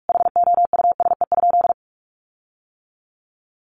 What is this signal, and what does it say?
Morse code, heard as a keyed tone, about 44 wpm, 700 Hz. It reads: HOUSE?
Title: Sentences from Top 1400 Words 45wpm